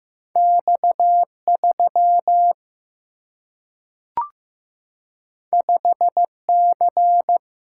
Morse code, heard as a keyed tone, about 15 wpm, 700 Hz. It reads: X 3 5 C
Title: Single-Letter Single-Number 15wpm